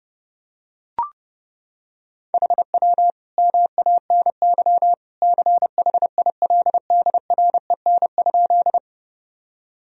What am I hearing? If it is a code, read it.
Morse code, heard as a keyed tone, about 32 wpm, 700 Hz. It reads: HW MANY CHILDREN?
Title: Sentences from Top 300 Words and Abbreviations 30wpm